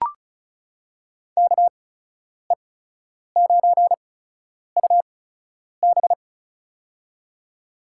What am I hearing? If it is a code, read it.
Morse code, heard as a keyed tone, about 36 wpm, spaced as for 10 wpm, 700 Hz. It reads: KE9UB
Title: USA Call Signs - 5 Characters - 8x Character Spacing - 35wpm